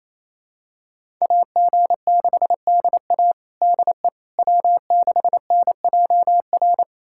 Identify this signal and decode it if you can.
Morse code, heard as a keyed tone, about 27 wpm, 700 Hz. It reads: AG6DA DE W6NJR
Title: USA Callsign DE Callsign - 5 Characters - 28wpm